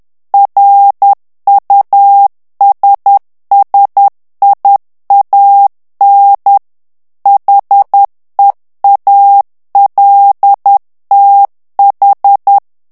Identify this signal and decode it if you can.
Morse code, heard as a keyed tone, about 10 wpm, 800 Hz. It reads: RUSSIAN HEALTH
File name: morse mp3 / ANGLAIS texte clair 2 vitesse 10 mots minute